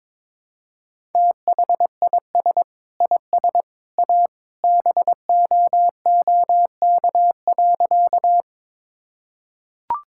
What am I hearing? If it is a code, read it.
Morse code, heard as a keyed tone, about 22 wpm, 700 Hz. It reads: THIS IS A BOOK.
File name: practice / 22wpm / Sentences with 3-Word Phrases - 22wpm